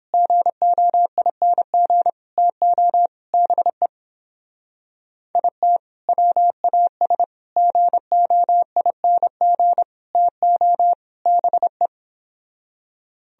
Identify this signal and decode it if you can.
Morse code, heard as a keyed tone, about 29 wpm, 700 Hz. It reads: GOING TO BE IT WAS GOING TO BE